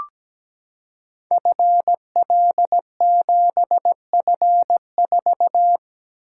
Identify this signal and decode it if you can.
Morse code, heard as a keyed tone, about 17 wpm, 700 Hz. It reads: FL7F4
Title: Code Groups - 5 Letter:Numbers - 17wpm